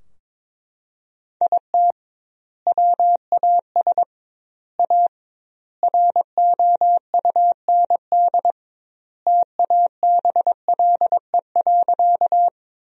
Morse code, heard as a keyed tone, about 22 wpm, 700 Hz. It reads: IT WAS A ROUND TABLE.
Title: 2x Word-Spacing - Sentences from Top 400 Words - 22wpm